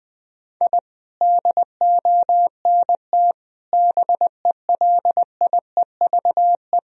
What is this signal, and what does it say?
Morse code, heard as a keyed tone, about 21 wpm, 700 Hz. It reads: I DONT BELIEVE
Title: Sentences from Top 1000 Words - Review Part 1 - 20wpm